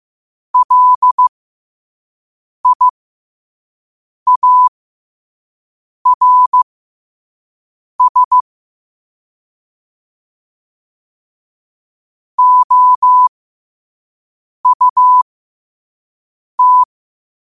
Morse code, heard as a keyed tone, about 15 wpm, spaced as for 5 wpm, 1000 Hz. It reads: LIARS OUT
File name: qotdfiles / 5wpm